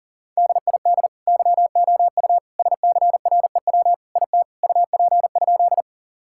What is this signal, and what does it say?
Morse code, heard as a keyed tone, about 40 wpm, 700 Hz. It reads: DID YOU SCREW IT UP?
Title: Sentences from Top 1600 Words 40wpm